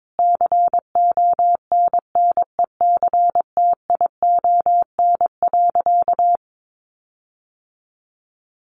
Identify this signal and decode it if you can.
Morse code, heard as a keyed tone, about 22 wpm, 700 Hz. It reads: CONNECTION.
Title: Sentences from Top 1300 Words 22wpm